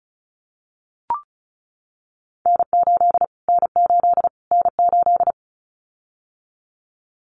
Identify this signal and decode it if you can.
Morse code, heard as a keyed tone, about 34 wpm, 700 Hz. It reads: N8 N8 N8
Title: Callsign Prefixes for Top 25 DXCC - Repeated 3x - 35wpm